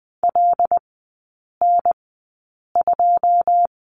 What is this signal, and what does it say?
Morse code, heard as a keyed tone, about 20 wpm, spaced as for 8 wpm, 700 Hz. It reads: LN2
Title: 081 Lesson 14e - Number 2 - Three Characters 20wpm